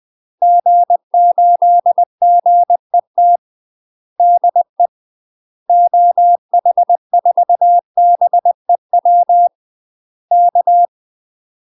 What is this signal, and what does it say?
Morse code, heard as a keyed tone, about 20 wpm, 700 Hz. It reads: G8GET DE OH4BEW K
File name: UK MASTER QSOs TRAINING / 20WPM / UK MASTER QSOs 0025